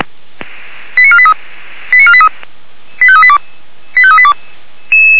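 Five-tone selective call (ZVEI), heard as quick runs of five tones, tones of about 70 ms. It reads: 89483 89483 85382 85382
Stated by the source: ZVEI five-tone call received by SDR.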